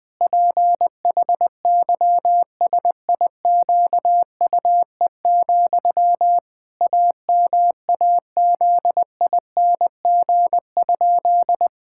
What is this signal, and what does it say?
Morse code, heard as a keyed tone, about 20 wpm, 700 Hz. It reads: PHYSIQUE, AMAZING?
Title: Pangram Challenge v2 - 20wpm